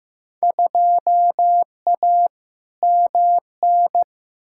Morse code, heard as a keyed tone, about 15 wpm, 700 Hz. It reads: 2A MN
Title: Field Day Exchanges - 15wpm